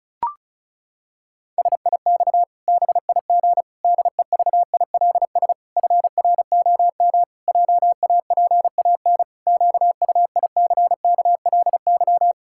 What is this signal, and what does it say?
Morse code, heard as a keyed tone, about 34 wpm, 700 Hz. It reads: SIX BIG DEVILS FROM JAPAN QUICKLY
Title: Pangram Challenge v2 - 35wpm